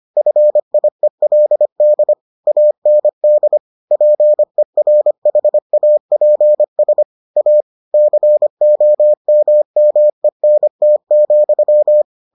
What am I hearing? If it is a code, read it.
Morse code, heard as a keyed tone, about 25 wpm, 600 Hz. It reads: FIELD AND PERHAPS A COMMENT,